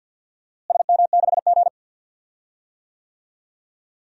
Morse code, heard as a keyed tone, about 48 wpm, 700 Hz. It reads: SM6G
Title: DXCC-100 Call Signs - 4 Characters - Repeated 2x - 50wpm